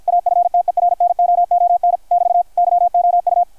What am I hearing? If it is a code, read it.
Morse code, heard as a keyed tone, about 53 wpm, 660 Hz, with radio noise. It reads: AFTERNOON <BT> YOU
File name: archived / CRSnet